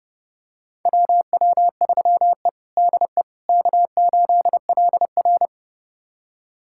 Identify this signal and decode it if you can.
Morse code, heard as a keyed tone, about 32 wpm, 700 Hz. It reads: WW3E DE K8LR
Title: USA Callsign DE Callsign - 4 Characters - 30wpm